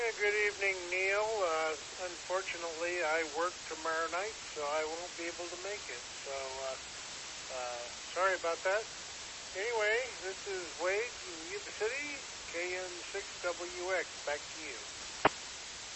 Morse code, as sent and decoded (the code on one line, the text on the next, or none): none